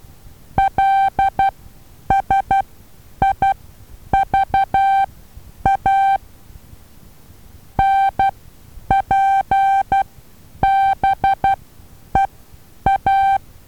Morse code, heard as a keyed tone, about 12 wpm, spaced as for 9 wpm, 780 Hz. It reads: LSIVA NPBEA
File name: Taviro HA6VA B